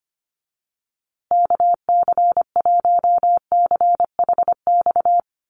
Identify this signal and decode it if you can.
Morse code, heard as a keyed tone, about 25 wpm, 700 Hz. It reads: KC1CHX
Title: USA Call Signs Repeated 3x 25wpm